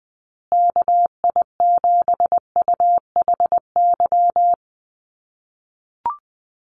Morse code, heard as a keyed tone, about 19 wpm, 700 Hz. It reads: KI7UHY
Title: USA Call Signs Repeated 2x 20wpm